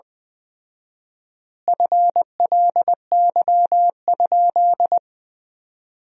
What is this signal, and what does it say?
Morse code, heard as a keyed tone, about 20 wpm, 700 Hz. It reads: FLY?
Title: 4x Word-Spacing - Sentences from Top 500 Words - 20wpm